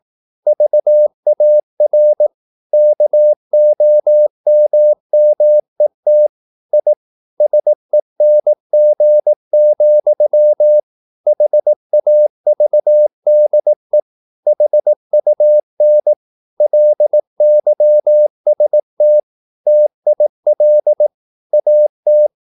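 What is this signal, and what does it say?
Morse code, heard as a keyed tone, about 18 wpm, 600 Hz. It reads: VAR KOMMET I SENG, HAVDE HUN LYST TIL AT
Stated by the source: Slow (QRS) Morse practice at 18 wpm, CW med Gnister.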